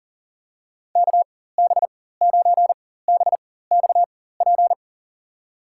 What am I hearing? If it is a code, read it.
Morse code, heard as a keyed tone, about 39 wpm, spaced as for 18 wpm, 700 Hz. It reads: KB9BXP
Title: USA Call Signs - 6 Characters - 4x Character Spacing - 40wpm